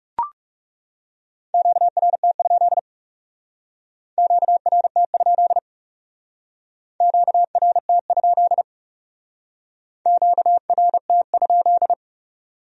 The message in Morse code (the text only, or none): QRT? QRT? QRT? QRT?